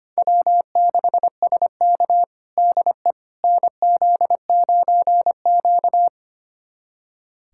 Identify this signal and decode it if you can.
Morse code, heard as a keyed tone, about 25 wpm, 700 Hz. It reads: W6SK DE NZ9Q
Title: USA Callsign DE Callsign - 4 Characters - 25wpm